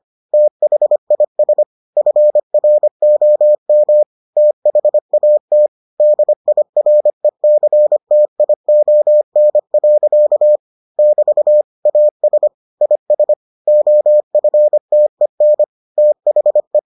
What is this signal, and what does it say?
Morse code, heard as a keyed tone, about 25 wpm, 600 Hz. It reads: THIS FROM THAT DIRECTION. <BT> AS IS OFTEN THE